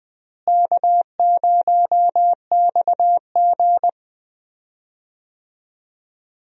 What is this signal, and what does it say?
Morse code, heard as a keyed tone, about 20 wpm, 700 Hz. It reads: K0XG
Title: USA Call Signs - 4 Characters - Repeated 3x - 20wpm